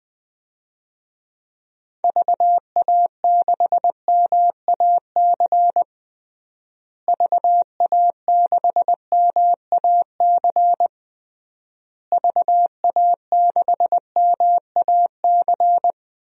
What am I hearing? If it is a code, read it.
Morse code, heard as a keyed tone, about 21 wpm, 700 Hz. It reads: VA6MAC VA6MAC VA6MAC
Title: DXCC-100 Call Signs - 6 Characters - Repeated 3x - 20wpm